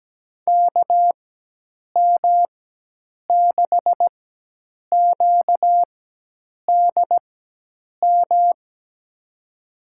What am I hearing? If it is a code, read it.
Morse code, heard as a keyed tone, about 17 wpm, spaced as for 8 wpm, 700 Hz. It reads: KM6QDM
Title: USA Call Signs - 4x Character Spacing - 17wpm